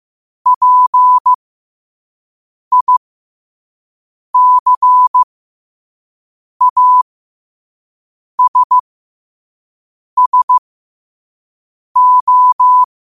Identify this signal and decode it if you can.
Morse code, heard as a keyed tone, about 15 wpm, spaced as for 5 wpm, 1000 Hz. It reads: PICASSO